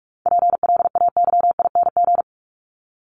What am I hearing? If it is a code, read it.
Morse code, heard as a keyed tone, about 44 wpm, 700 Hz. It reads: PLAYING
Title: ING Suffix - Top 100 Words with Base Words 45wpm